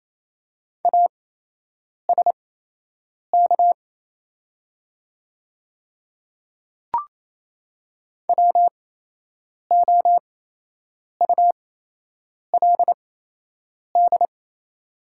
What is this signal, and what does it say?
Morse code, heard as a keyed tone, about 28 wpm, spaced as for 8 wpm, 690 Hz. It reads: ASK WOULD
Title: 8x Character Spacing - Top 300 Words - 28wpm